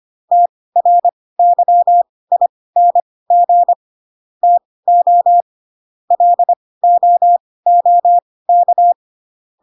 Morse code, heard as a keyed tone, about 25 wpm, spaced as for 18 wpm, 700 Hz. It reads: TRYING TO LOOK